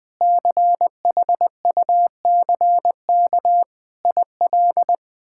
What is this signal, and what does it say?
Morse code, heard as a keyed tone, about 21 wpm, 700 Hz. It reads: CHUCK IL
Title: K1USN SST QSO Practice - 20wpm